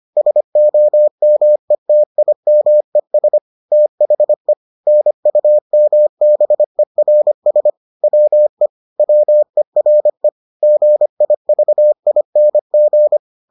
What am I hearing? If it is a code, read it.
Morse code, heard as a keyed tone, about 25 wpm, 600 Hz. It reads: SOMETIMES THE NUMBERS WE WERE GIVING